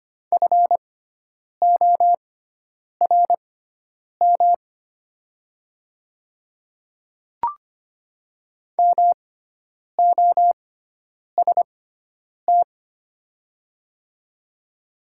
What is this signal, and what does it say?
Morse code, heard as a keyed tone, about 25 wpm, spaced as for 9 wpm, 700 Hz. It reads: FORM MOST
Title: 6x Character Spacing - Top 200 Words - 25wpm